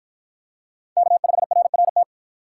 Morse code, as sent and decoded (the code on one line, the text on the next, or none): -.- ..... .-. .-. -
K5RRT